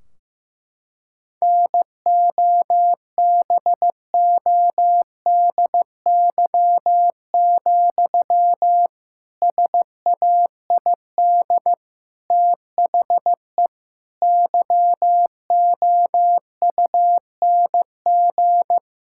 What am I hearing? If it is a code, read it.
Morse code, heard as a keyed tone, about 16 wpm, 700 Hz. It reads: NOBODY, SAID THE YOUNG
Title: Sentences from Top 1600 Words 15wpm